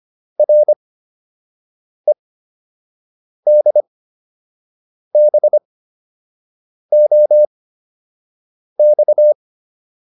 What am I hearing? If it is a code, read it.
Morse code, heard as a keyed tone, about 25 wpm, spaced as for 6 wpm, 600 Hz. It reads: REDBOX